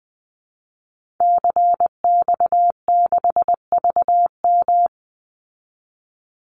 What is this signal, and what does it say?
Morse code, heard as a keyed tone, about 20 wpm, 700 Hz. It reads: CX6VM
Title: Call Signs Repeated 2x - 20wpm